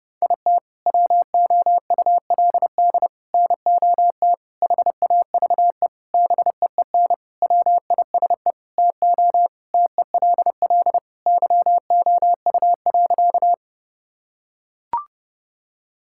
Morse code, heard as a keyed tone, about 32 wpm, 700 Hz. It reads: IT WOULD NOT HAVE BEEN WISE TO TELL YOU.
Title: Sentences from Top 1600 Words 30wpm